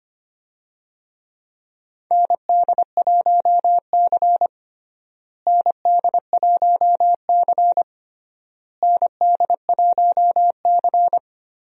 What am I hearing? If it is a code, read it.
Morse code, heard as a keyed tone, about 25 wpm, 700 Hz. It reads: ND1C ND1C ND1C
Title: USA Call Signs - 4 Characters - Repeated 3x - 25wpm